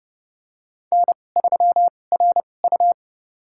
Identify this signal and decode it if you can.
Morse code, heard as a keyed tone, about 29 wpm, spaced as for 21 wpm, 700 Hz. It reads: N3RU
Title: USA Call Signs - 4 Characters - 2x Character Spacing - 30wpm